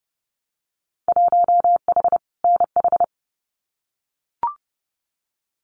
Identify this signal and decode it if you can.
Morse code, heard as a keyed tone, about 32 wpm, 700 Hz. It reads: 1H NH
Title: Winter Field Day Exchanges 30wpm